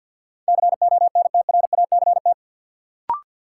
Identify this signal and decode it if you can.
Morse code, heard as a keyed tone, about 53 wpm, 700 Hz. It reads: CONTRACT
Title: Sentences from Top 1400 Words - New Word Drill - 50wpm